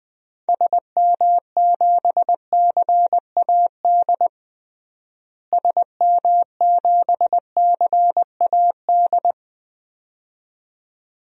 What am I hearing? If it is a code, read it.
Morse code, heard as a keyed tone, about 20 wpm, 690 Hz. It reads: SM7CAD SM7CAD
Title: DXCC-100 Call Signs - 6 Characters - Repeated 2x - 20wpm